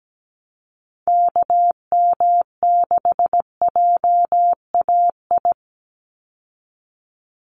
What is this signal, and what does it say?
Morse code, heard as a keyed tone, about 17 wpm, 700 Hz. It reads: KM6JAI
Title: USA Call Signs 17wpm